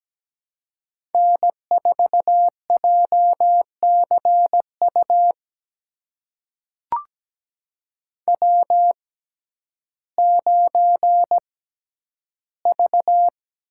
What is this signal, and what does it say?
Morse code, heard as a keyed tone, about 17 wpm, spaced as for 10 wpm, 700 Hz. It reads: N4JCU W9V
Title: USA Call Signs - 5 Characters - 6x Character Spacing - 17wpm